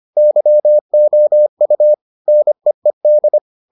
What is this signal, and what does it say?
Morse code, heard as a keyed tone, about 25 wpm, 600 Hz. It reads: YOU NEED